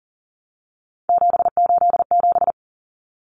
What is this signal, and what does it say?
Morse code, heard as a keyed tone, about 40 wpm, 700 Hz. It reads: 787
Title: Common Cut Numbers - 3-Digit 40wpm